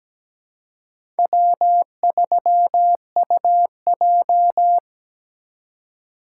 Morse code, heard as a keyed tone, about 17 wpm, 700 Hz. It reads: W3UJ
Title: USA Call Signs - 4 Characters - 3x Character Spacing - 17wpm